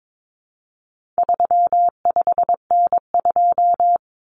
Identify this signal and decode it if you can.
Morse code, heard as a keyed tone, about 22 wpm, 700 Hz. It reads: 35N2
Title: Common Cut Numbers - 4-Digit 22wpm